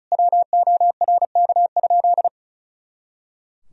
Morse code, heard as a keyed tone, about 36 wpm, 700 Hz. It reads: WORK?
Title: Sentences from Top 500 Words and Abbreviations 35wpm